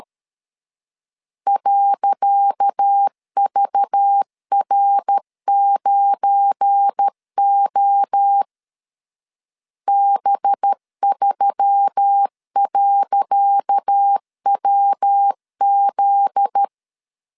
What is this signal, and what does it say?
Morse code, heard as a keyed tone, about 13 wpm, 800 Hz. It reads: .VR9O B3.WZ
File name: LEZIONI / 10 SIGNS / 10C DOT 12wpm 2451